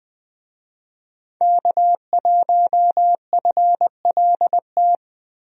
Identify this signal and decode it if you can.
Morse code, heard as a keyed tone, about 20 wpm, 700 Hz. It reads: K1FLT
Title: USA Call Signs - 5 Characters - 20wpm